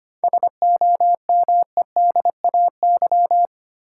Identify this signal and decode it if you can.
Morse code, heard as a keyed tone, about 25 wpm, 700 Hz. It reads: SOMEDAY
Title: Sentences from Top 1600 Words - New Word Drill - 25wpm